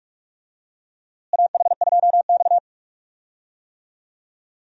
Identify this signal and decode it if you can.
Morse code, heard as a keyed tone, about 46 wpm, 700 Hz. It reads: AH2X
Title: USA Call Signs - 4 Characters - Repeated 2x - 45wpm